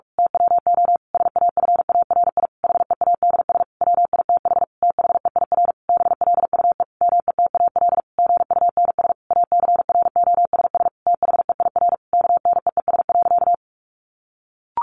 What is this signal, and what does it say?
Morse code, heard as a keyed tone, about 44 wpm, 700 Hz. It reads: TWO SAFARI HEADS WITH THEIR BLUE METAL GUNS ACROSS THEIR KNEES.
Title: A Sound Of Thunder 45wpm